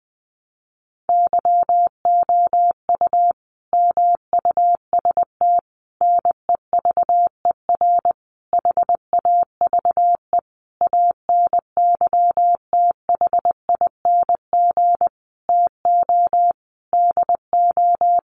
Morse code, heard as a keyed tone, about 20 wpm, 700 Hz. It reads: YOU MUST NEVER HAVE ANYTHING TO DO
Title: Sentences from Top 1000 Words 20wpm